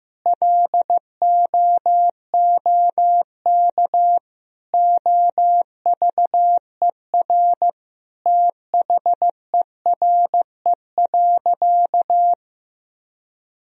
Morse code, heard as a keyed tone, about 15 wpm, 700 Hz. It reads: LOOK OVER THERE.
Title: Sentences from Top 1000 Words - Review Part 2 - 15wpm